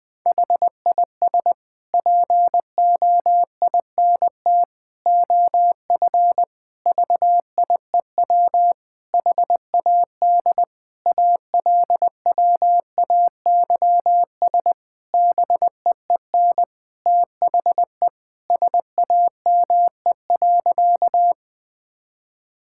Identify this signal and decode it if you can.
Morse code, heard as a keyed tone, about 20 wpm, 700 Hz. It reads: HIS POINT OF VIEW HAD ALWAYS BEEN THE SAME.
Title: Sentences from Top 1000 Words - Review Part 6 - 20wpm